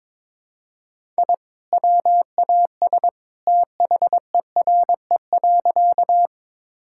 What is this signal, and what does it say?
Morse code, heard as a keyed tone, about 22 wpm, 700 Hz. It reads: I WAS THERE.
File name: practice / 22wpm / Sentences with 2-Word Phrases - 22wpm